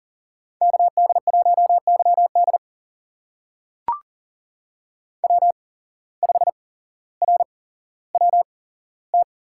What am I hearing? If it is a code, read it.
Morse code, heard as a keyed tone, about 39 wpm, spaced as for 17 wpm, 700 Hz. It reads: KD1YD W5RWT
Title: USA Call Signs - 5 Characters - 8x Character Spacing - 40wpm